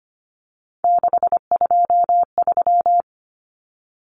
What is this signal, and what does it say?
Morse code, heard as a keyed tone, about 25 wpm, 700 Hz. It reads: 623
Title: Common Cut Numbers - 3-Digit 25wpm